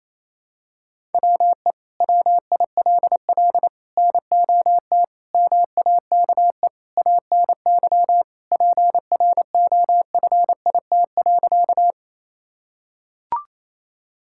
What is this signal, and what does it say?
Morse code, heard as a keyed tone, about 28 wpm, 700 Hz. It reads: WE WILL NOT MAKE ANY PROFIT.
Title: Sentences from Top 1000 Words - Review Part 5 - 28wpm